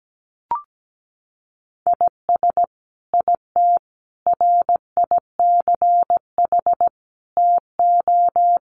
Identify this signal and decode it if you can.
Morse code, heard as a keyed tone, about 17 wpm, 700 Hz. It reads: IS IT RICH TO
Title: Sentences from Top 1100 Words 17wpm